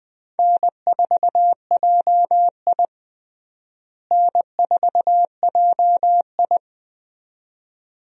Morse code, heard as a keyed tone, about 20 wpm, 700 Hz. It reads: N4JI N4JI
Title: USA Call Signs - 4 Characters - Repeated 2x - 20wpm